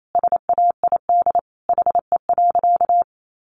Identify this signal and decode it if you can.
Morse code, heard as a keyed tone, about 28 wpm, 700 Hz. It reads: SAID HE.
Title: Sentences from Top 400 Words 28wpm